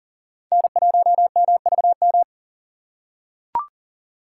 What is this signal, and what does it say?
Morse code, heard as a keyed tone, about 38 wpm, 700 Hz. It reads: N1MVM
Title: USA Call Signs - 5 Characters - 3x Character Spacing - 40wpm